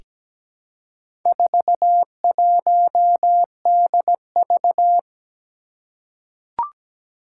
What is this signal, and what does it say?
Morse code, heard as a keyed tone, about 17 wpm, 700 Hz. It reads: 41DV
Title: Four Challenging Letter-Number Combinations 17wpm